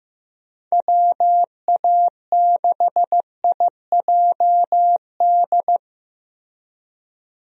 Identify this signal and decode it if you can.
Morse code, heard as a keyed tone, about 16 wpm, 700 Hz. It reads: WA6IJD
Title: USA Call Signs - 6 Characters - Repeated 3x - 15wpm